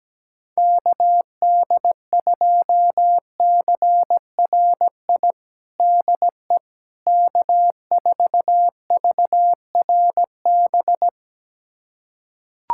Morse code, heard as a keyed tone, about 17 wpm, 700 Hz. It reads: KD2CRI DE K4VRB
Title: USA Callsign DE Callsign 17wpm